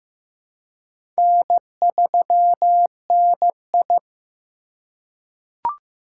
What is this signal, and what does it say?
Morse code, heard as a keyed tone, about 15 wpm, 700 Hz. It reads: N3NI
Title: USA Call Signs - 4x Character Spacing - 15wpm